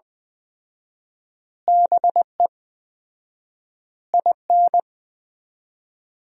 Morse code, heard as a keyed tone, about 19 wpm, 700 Hz. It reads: BE IN
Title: 4x Word-Spacing - Sentences from Top 300 Words - 20wpm